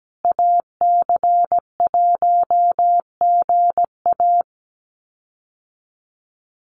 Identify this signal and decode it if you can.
Morse code, heard as a keyed tone, about 17 wpm, 700 Hz. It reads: AC1GA
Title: USA Call Signs Repeated 2x 17wpm